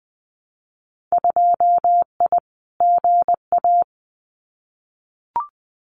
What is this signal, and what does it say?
Morse code, heard as a keyed tone, about 20 wpm, 700 Hz. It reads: 2I GA
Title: Winter Field Day Exchanges 20wpm